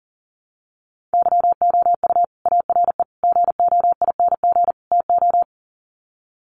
Morse code, heard as a keyed tone, about 40 wpm, 700 Hz. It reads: YOU ARE GOING TO
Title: Sets of 4 Words 40wpm